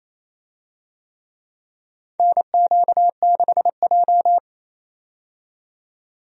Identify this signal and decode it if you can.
Morse code, heard as a keyed tone, about 28 wpm, 700 Hz. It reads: NQ6J
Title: USA Call Signs - 4 Characters - 28wpm